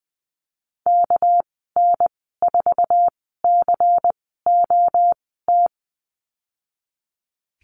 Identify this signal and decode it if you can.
Morse code, heard as a keyed tone, about 20 wpm, spaced as for 14 wpm, 700 Hz. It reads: KN4COT
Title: 219 Lesson 35g - Number 0 - Call Signs 20wpm